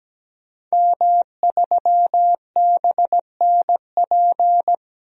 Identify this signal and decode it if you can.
Morse code, heard as a keyed tone, about 17 wpm, 700 Hz. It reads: M3BNP
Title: DXCC-100 Call Signs - 5 Characters - 17wpm